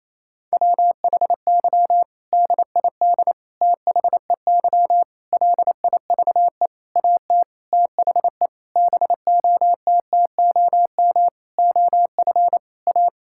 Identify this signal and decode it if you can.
Morse code, heard as a keyed tone, about 28 wpm, 700 Hz. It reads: WHY DID THEY LIVE AT THE BOTTOM OF A
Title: Sentences from Top 1000 Words - Review Part 6 - 28wpm